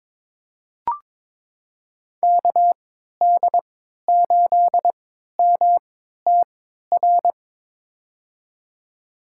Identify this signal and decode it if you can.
Morse code, heard as a keyed tone, about 21 wpm, spaced as for 12 wpm, 700 Hz. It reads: KD8MTR
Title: USA Call Signs - 6 Characters - 3x Character Spacing - 22wpm